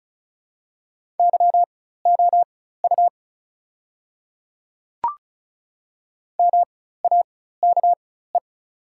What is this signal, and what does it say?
Morse code, heard as a keyed tone, about 36 wpm, spaced as for 16 wpm, 700 Hz. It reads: YOU MAKE
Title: 4x Character Spacing - Top 100 Words - 35wpm